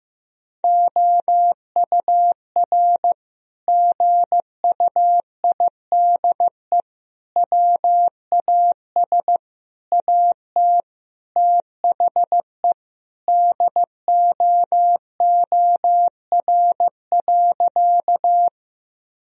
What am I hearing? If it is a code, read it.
Morse code, heard as a keyed tone, about 15 wpm, 700 Hz. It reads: OUR GUIDE WAS AT THE DOOR.
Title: Sentences from Top 1000 Words - Review Part 3 - 15wpm